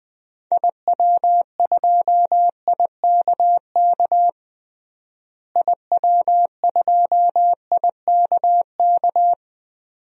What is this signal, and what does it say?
Morse code, heard as a keyed tone, about 20 wpm, 700 Hz. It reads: IW2IKK IW2IKK
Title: DXCC-100 Call Signs - 6 Characters - Repeated 2x - 20wpm